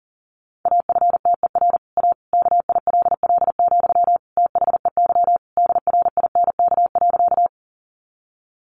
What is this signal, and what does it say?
Morse code, heard as a keyed tone, about 40 wpm, 700 Hz. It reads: AFTER A KILL, THEY DRINK.
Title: Sentences from Top 1500 Words 40wpm